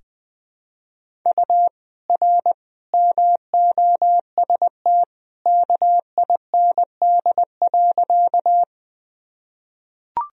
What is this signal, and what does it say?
Morse code, heard as a keyed tone, about 20 wpm, 700 Hz. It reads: U R MOST KIND.
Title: Sentences from Top 200 Words and Abbreviations 20wpm